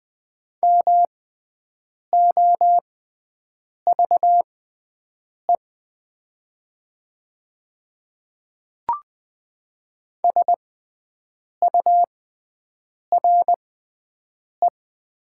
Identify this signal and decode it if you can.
Morse code, heard as a keyed tone, about 20 wpm, spaced as for 7 wpm, 700 Hz. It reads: MOVE SURE
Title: 6x Character Spacing - Top 500 Words - 20wpm